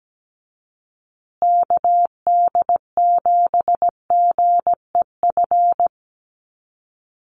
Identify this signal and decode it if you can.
Morse code, heard as a keyed tone, about 17 wpm, 700 Hz. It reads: KD7GEF
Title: USA Call Signs Repeated 3x 17wpm